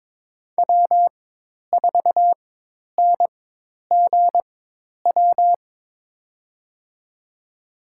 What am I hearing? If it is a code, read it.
Morse code, heard as a keyed tone, about 22 wpm, spaced as for 10 wpm, 700 Hz. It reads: W4NGW
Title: USA Call Signs - 5 Characters - 4x Character Spacing - 22wpm